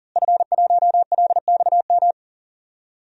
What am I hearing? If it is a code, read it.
Morse code, heard as a keyed tone, about 41 wpm, 700 Hz. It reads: F1LXM
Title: DXCC-100 Call Signs - 5 Characters - Repeated 2x - 40wpm